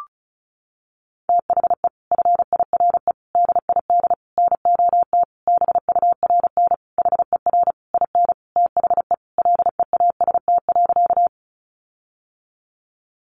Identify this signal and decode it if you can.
Morse code, heard as a keyed tone, about 34 wpm, 700 Hz. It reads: THE FIRE DID NOT BURN HER IN THE LEAST.
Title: Sentences from Top 1300 Words 35wpm